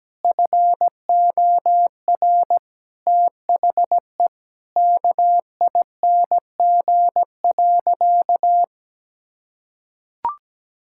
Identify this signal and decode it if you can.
Morse code, heard as a keyed tone, about 17 wpm, 700 Hz. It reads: FOR THE KING.
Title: Sentences from Top 1000 Words - Review Part 6 - 17wpm